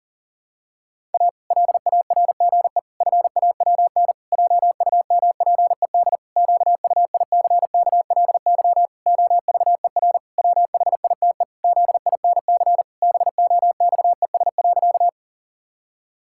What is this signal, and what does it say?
Morse code, heard as a keyed tone, about 40 wpm, 700 Hz. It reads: A LARGE FAWN JUMPED QUICKLY OVER WHITE ZINC BOXES.
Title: Pangram Challenge v2 - 40wpm